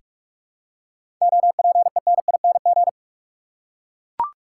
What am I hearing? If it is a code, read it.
Morse code, heard as a keyed tone, about 44 wpm, 700 Hz. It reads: OPENING
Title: Seven Letter Words - 45wpm